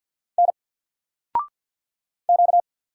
Morse code, heard as a keyed tone, about 50 wpm, 700 Hz. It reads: N Q
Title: Single Character (Letter:Number:Punctuation:Prosign) 50wpm